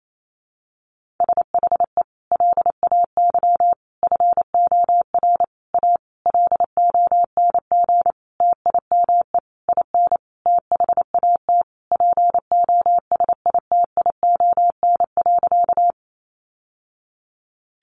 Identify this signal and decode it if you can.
Morse code, heard as a keyed tone, about 28 wpm, 700 Hz. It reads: SHE LAY FOR A LONG TIME IN THAT POSITION.
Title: Sentences from Top 900 Words 28wpm